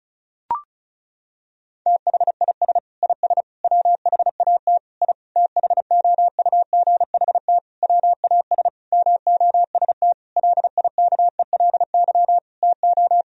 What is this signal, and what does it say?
Morse code, heard as a keyed tone, about 34 wpm, 700 Hz. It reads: THIS IS WHAT I THOUGHT WAS MOST LIKELY TO
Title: Sentences from Top 1000 Words - Review Part 4 - 35wpm